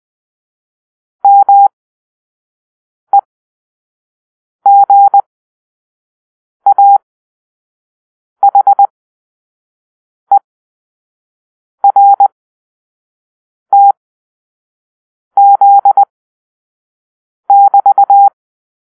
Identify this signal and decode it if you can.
Morse code, heard as a keyed tone, about 20 wpm, spaced as for 5 wpm, 800 Hz. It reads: MEGAHERTZ<BT>
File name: arrl / Grp-5 Drill-4